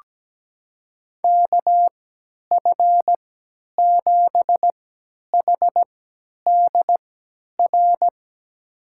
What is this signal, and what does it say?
Morse code, heard as a keyed tone, about 17 wpm, spaced as for 10 wpm, 700 Hz. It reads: KF7HDR
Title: USA Call Signs - 6 Characters - 3x Character Spacing - 17wpm